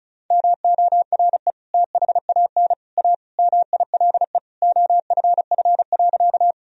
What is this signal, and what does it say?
Morse code, heard as a keyed tone, about 36 wpm, 700 Hz. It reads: MORE THAN A MILE OFF.
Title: Sentences from Top 1000 Words - Review Part 1 - 35wpm